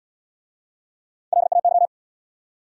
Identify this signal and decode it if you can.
Morse code, heard as a keyed tone, about 77 wpm, 700 Hz. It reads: LAY